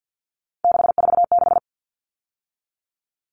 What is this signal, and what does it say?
Morse code, heard as a keyed tone, about 50 wpm, 700 Hz. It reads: 646
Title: All Cut Numbers - 3-Digit 50wpm